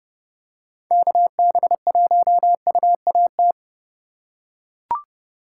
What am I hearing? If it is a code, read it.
Morse code, heard as a keyed tone, about 30 wpm, 700 Hz. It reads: KB1UAT